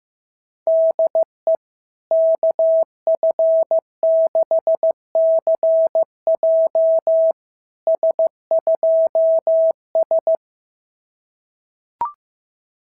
Morse code, heard as a keyed tone, about 15 wpm, 660 Hz. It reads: DE KF6CJ S2S
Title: SOTA QSO Practice 15wpm